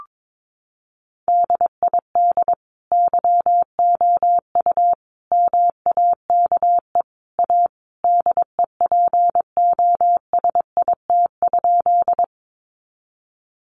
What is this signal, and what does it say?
Morse code, heard as a keyed tone, about 22 wpm, 700 Hz. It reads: DID YOU MAKE A DEPOSIT?
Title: Sentences from Top 1300 Words 22wpm